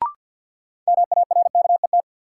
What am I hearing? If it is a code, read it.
Morse code, heard as a keyed tone, about 52 wpm, 700 Hz. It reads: MARKET
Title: Sentences from Top 600 Words - New Word Drill - 50wpm